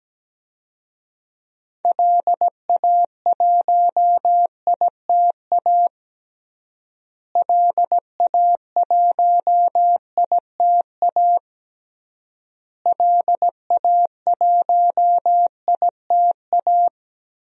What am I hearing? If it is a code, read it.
Morse code, heard as a keyed tone, about 17 wpm, 700 Hz. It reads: LA1ITA LA1ITA LA1ITA
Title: DXCC-100 Call Signs - 6 Characters - Repeated 3x - 17wpm